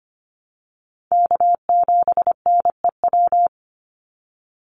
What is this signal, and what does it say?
Morse code, heard as a keyed tone, about 25 wpm, 700 Hz. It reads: K7NEW
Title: NA SOTA Chaser Callsigns 25wpm